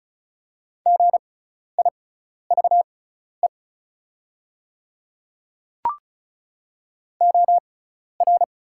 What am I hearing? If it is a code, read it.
Morse code, heard as a keyed tone, about 36 wpm, spaced as for 12 wpm, 690 Hz. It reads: GIVE OR
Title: 6x Character Spacing - Top 200 Words - 35wpm